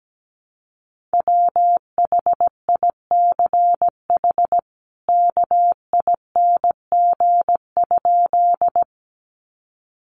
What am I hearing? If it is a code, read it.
Morse code, heard as a keyed tone, about 17 wpm, 700 Hz. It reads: WHICH KING?
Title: Sentences from Top 400 Words 17wpm